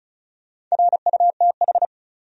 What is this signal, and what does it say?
Morse code, heard as a keyed tone, about 34 wpm, 700 Hz. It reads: RUTH
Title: USA Common Names - Max 5 Letters - 35wpm